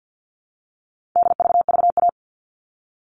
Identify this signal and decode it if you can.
Morse code, heard as a keyed tone, about 50 wpm, 700 Hz. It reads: DVVA
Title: All Cut Numbers - 4-Digit 50wpm